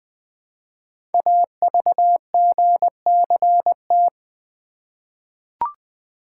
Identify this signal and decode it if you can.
Morse code, heard as a keyed tone, about 20 wpm, 700 Hz. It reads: AVGCT
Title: Code Groups - 5 Letter:Numbers - 20wpm